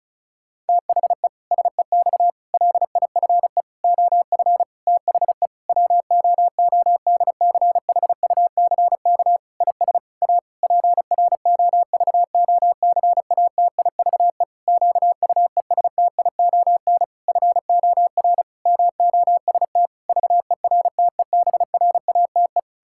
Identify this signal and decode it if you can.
Morse code, heard as a keyed tone, about 36 wpm, 700 Hz. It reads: THE SEX LIFE OF THE WOODCHUCK IS A PROVOCATIVE QUESTION FOR MOST VERTEBRATE
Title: Pangram Challenge v2 - 35wpm